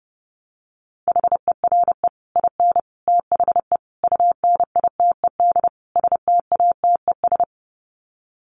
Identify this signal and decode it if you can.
Morse code, heard as a keyed tone, about 30 wpm, 700 Hz. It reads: HERE IN THE UNITED STATES
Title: Sets of 5 Words 30wpm